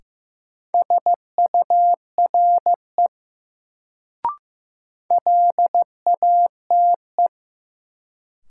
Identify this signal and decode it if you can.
Morse code, heard as a keyed tone, about 15 wpm, 700 Hz. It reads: SURE LATE
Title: Sentences from Top 300 Words - New Word Drill - 15wpm